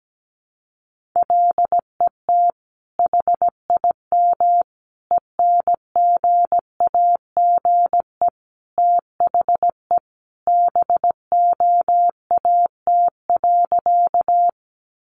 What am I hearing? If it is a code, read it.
Morse code, heard as a keyed tone, about 17 wpm, 700 Hz. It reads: LET HIM ENGAGE THE BOAT.
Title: Sentences from Top 1500 Words 17wpm